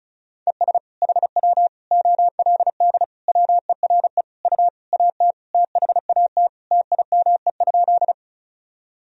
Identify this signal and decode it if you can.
Morse code, heard as a keyed tone, about 36 wpm, 700 Hz. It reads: ES HW OLD WERE U AT THAT TIME?
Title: Sentences from Top 200 Words and Abbreviations 35wpm